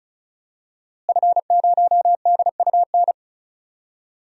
Morse code, heard as a keyed tone, about 34 wpm, 700 Hz. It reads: F0DUN
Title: DXCC-100 Call Signs - 5 Characters - Repeated 2x - 35wpm